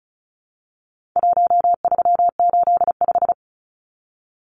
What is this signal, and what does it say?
Morse code, heard as a keyed tone, about 34 wpm, 700 Hz. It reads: 1385
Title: Numbers - 4-Digit 35wpm